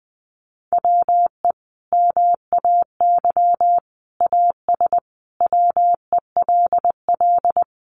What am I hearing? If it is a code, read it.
Morse code, heard as a keyed tone, about 20 wpm, 700 Hz. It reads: WE MAY AS WELL
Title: Sentences from Top 800 Words 20wpm